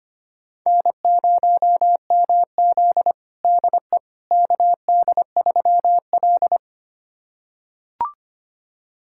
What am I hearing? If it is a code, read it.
Morse code, heard as a keyed tone, about 25 wpm, 700 Hz. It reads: N0MZ DE KD3L
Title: USA Callsign DE Callsign - 4 Characters - 25wpm